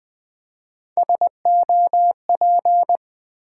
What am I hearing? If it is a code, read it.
Morse code, heard as a keyed tone, about 20 wpm, 700 Hz. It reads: SOP
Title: Build A Word - 3 Letters At A Time - 9-Letter Words - 20wpm